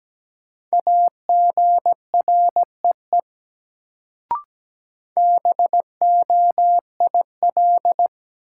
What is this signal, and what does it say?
Morse code, heard as a keyed tone, about 17 wpm, 700 Hz. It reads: AGREE BOIL
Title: Sets of 5 Words Encore - New Word Drill - 17wpm